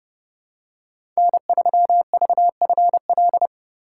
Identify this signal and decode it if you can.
Morse code, heard as a keyed tone, about 30 wpm, 700 Hz. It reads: N3VFL
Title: USA Call Signs - 5 Characters - 6x Character Spacing - 30wpm